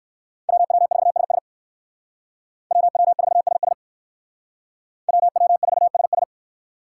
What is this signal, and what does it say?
Morse code, heard as a keyed tone, about 58 wpm, 700 Hz. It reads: WW4SS WW4SS WW4SS